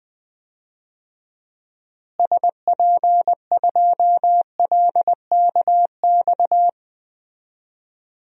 Morse code, heard as a keyed tone, about 20 wpm, 700 Hz. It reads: SP2LKX
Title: DXCC-100 Call Signs - 6 Characters - Repeated 3x - 20wpm